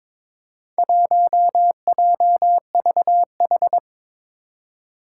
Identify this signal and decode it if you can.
Morse code, heard as a keyed tone, about 22 wpm, 700 Hz. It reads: 1JVH
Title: Four Challenging Letter-Number Combinations 22wpm